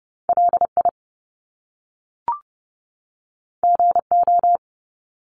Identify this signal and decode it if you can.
Morse code, heard as a keyed tone, about 30 wpm, 700 Hz. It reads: L I G O
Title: Top Two Letter Combinations in Words 30wpm